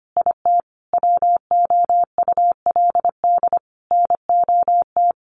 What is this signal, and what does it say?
Morse code, heard as a keyed tone, about 25 wpm, 700 Hz. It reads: IT WOULD NOT
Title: Sentences from Top 1300 Words 25wpm